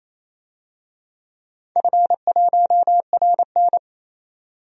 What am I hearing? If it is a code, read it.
Morse code, heard as a keyed tone, about 28 wpm, 700 Hz. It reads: F1RN
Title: DXCC-100 Call Signs - 4 Characters - 28wpm